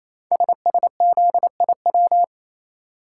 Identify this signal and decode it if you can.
Morse code, heard as a keyed tone, about 27 wpm, 700 Hz. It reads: SSZIW
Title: Code Groups - 5 Letters - 28wpm